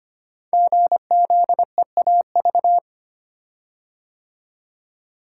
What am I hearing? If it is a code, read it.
Morse code, heard as a keyed tone, about 25 wpm, 700 Hz. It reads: GZEAV
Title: Code Groups - 5 Letter:Numbers - 25wpm